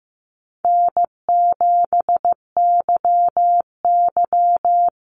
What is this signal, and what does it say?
Morse code, heard as a keyed tone, about 15 wpm, 700 Hz. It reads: N7YY
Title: Easy Call Signs - 15wpm